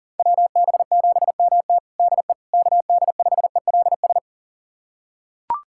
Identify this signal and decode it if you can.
Morse code, heard as a keyed tone, about 40 wpm, 700 Hz. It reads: WB7MT DE KD5ELS
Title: USA Callsign DE Callsign 40wpm